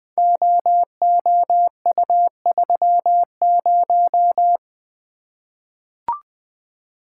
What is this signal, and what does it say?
Morse code, heard as a keyed tone, about 20 wpm, 700 Hz. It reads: OOU30
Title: Code Groups - 5 Letter:Numbers - 20wpm